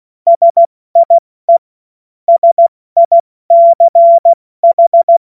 Morse code, heard as a keyed tone, about 16 wpm, 680 Hz. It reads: SIE SICH
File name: Morsekurs / Witze / Witz 20 Februar 2024 70 F80